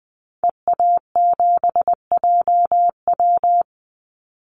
Morse code, heard as a keyed tone, about 20 wpm, 700 Hz. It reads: EA7JW
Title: Call Signs Repeated 2x - 20wpm